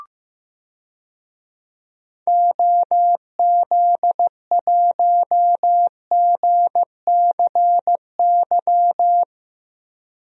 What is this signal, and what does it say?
Morse code, heard as a keyed tone, about 15 wpm, 700 Hz. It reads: OZ1GCY
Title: DXCC-100 Call Signs - 6 Characters - Repeated 3x - 15wpm